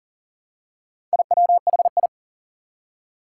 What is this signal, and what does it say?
Morse code, heard as a keyed tone, about 39 wpm, 700 Hz. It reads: IWHI
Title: Four Challenging Letter-Number Combinations 40wpm